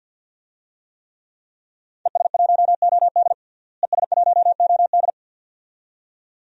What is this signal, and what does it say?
Morse code, heard as a keyed tone, about 51 wpm, 700 Hz. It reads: ES1OD ES1OD
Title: DXCC-100 Call Signs - 5 Characters - Repeated 2x - 50wpm